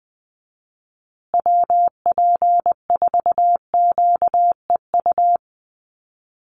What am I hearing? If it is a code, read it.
Morse code, heard as a keyed tone, about 20 wpm, 700 Hz. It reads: WP4QEU
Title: USA Call Signs Repeated 3x 20wpm